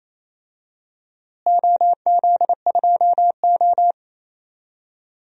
Morse code, heard as a keyed tone, about 28 wpm, 700 Hz. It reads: OZ2O
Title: DXCC-100 Call Signs - 4 Characters - Repeated 2x - 28wpm